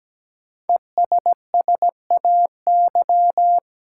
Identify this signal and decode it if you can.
Morse code, heard as a keyed tone, about 17 wpm, 700 Hz. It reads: ESSAY
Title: Sentences from Top 1500 Words - New Word Drill Alt - 17wpm